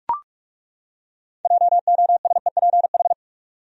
Morse code, heard as a keyed tone, about 42 wpm, 700 Hz. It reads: JOSEPH
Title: USA Common Names - 45wpm